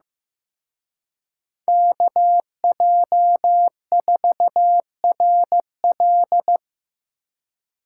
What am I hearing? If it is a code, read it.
Morse code, heard as a keyed tone, about 18 wpm, 700 Hz. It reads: KJ4RL
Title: Speed Racing - USA 2x2 Call Signs 15wpm